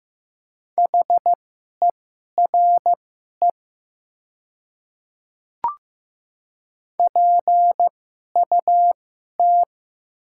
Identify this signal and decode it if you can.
Morse code, heard as a keyed tone, about 15 wpm, spaced as for 11 wpm, 700 Hz. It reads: HERE PUT
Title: 2x Character Spacing - Top 200 Words - 15wpm